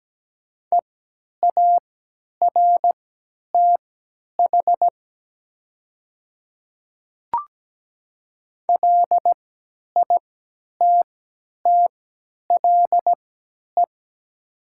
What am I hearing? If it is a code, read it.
Morse code, heard as a keyed tone, about 17 wpm, spaced as for 10 wpm, 700 Hz. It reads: EARTH LITTLE
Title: 3x Character Spacing - Top 200 Words - 17wpm